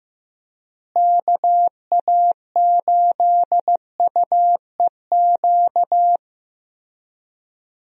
Morse code, heard as a keyed tone, about 15 wpm, 700 Hz. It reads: KA8UEQ
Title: USA Call Signs - 6 Characters - Repeated 3x - 15wpm